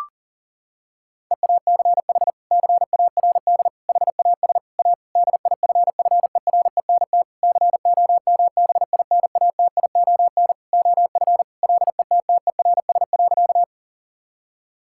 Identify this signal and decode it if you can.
Morse code, heard as a keyed tone, about 40 wpm, 700 Hz. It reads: EACH CARD HAS A DIFFERENT COMBINATION OF LETTERS.
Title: Sentences from Top 1000 Words - Review Part 6 - 40wpm